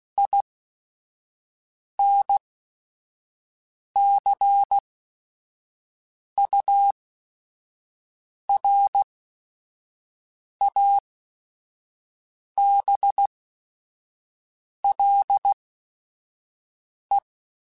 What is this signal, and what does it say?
Morse code, heard as a keyed tone, about 16 wpm, spaced as for 5 wpm, 800 Hz. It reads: INCURABLE
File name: Quotes / Quotes-mp3s / Optimist @5